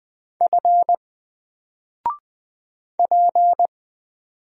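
Morse code, heard as a keyed tone, about 20 wpm, 700 Hz. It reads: F P
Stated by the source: Single letter and number Morse drill at 20 wpm, from N4LCD.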